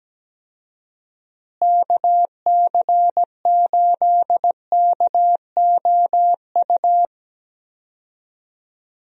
Morse code, heard as a keyed tone, about 17 wpm, 700 Hz. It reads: KC8KOU
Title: USA Call Signs - 6 Characters - Repeated 3x - 17wpm